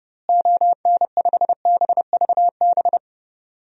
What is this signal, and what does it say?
Morse code, heard as a keyed tone, about 30 wpm, 700 Hz. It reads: ON5BVB
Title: DXCC-100 Call Signs - 6 Characters - Repeated 3x - 30wpm